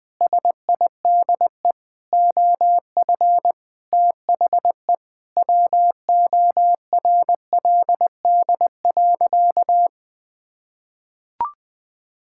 Morse code, heard as a keyed tone, about 21 wpm, 700 Hz. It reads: SIDE OF THE WORLD.
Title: Sentences with 7-Word Phrases - Spoken Beginning - 20wpm